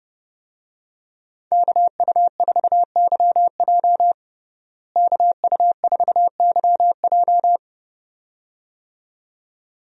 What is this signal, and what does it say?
Morse code, heard as a keyed tone, about 30 wpm, 700 Hz. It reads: KU4YJ KU4YJ
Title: USA Call Signs - 5 Characters - Repeated 2x - 30wpm